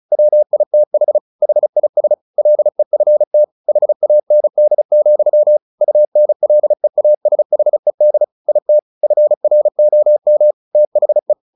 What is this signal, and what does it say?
Morse code, heard as a keyed tone, about 34 wpm, 600 Hz. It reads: WITH HIS LEFT HAND, UNLEASHED IT FROM THE
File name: the old man and the sea / the old man and the sea 1 35wpm